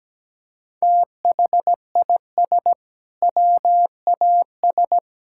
Morse code, heard as a keyed tone, about 17 wpm, 700 Hz. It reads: THIS WAS
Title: Sentences from Top 1000 Words - Review Part 5 - 17wpm